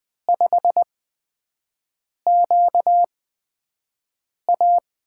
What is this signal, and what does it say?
Morse code, heard as a keyed tone, about 20 wpm, spaced as for 5 wpm, 700 Hz. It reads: 5QA
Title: USA Call Signs - 8x Character Spacing - 20wpm